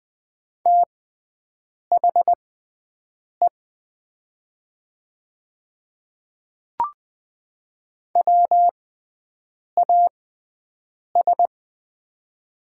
Morse code, heard as a keyed tone, about 20 wpm, spaced as for 7 wpm, 700 Hz. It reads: THE WAS
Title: 6x Character Spacing - Top 100 Words - 20wpm